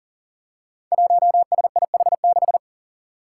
Morse code, heard as a keyed tone, about 42 wpm, 700 Hz. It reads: 1SIH6
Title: Five Challenging Letter-Number Combinations 40wpm